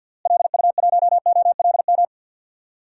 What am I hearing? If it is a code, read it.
Morse code, heard as a keyed tone, about 52 wpm, 700 Hz. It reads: LU1OLM
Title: DXCC-100 Call Signs - 6 Characters - Repeated 3x - 50wpm